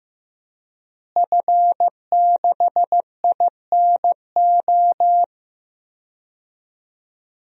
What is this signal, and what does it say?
Morse code, heard as a keyed tone, about 15 wpm, 700 Hz. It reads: F6INO
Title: DXCC-100 Call Signs - 5 Characters - Repeated 2x - 15wpm